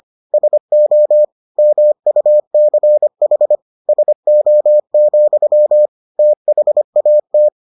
Morse code, heard as a keyed tone, about 25 wpm, 600 Hz. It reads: SO MUCH SO, THAT